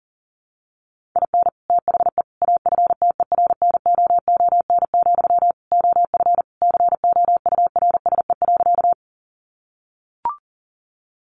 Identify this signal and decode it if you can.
Morse code, heard as a keyed tone, about 40 wpm, 700 Hz. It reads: IN THE AFTERNOON, OF COURSE.
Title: Sentences from Top 1500 Words 40wpm